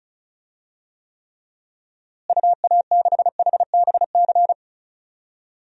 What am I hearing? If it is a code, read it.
Morse code, heard as a keyed tone, about 34 wpm, 700 Hz. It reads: UA6HBC
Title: DXCC-100 Call Signs - 6 Characters - Repeated 2x - 35wpm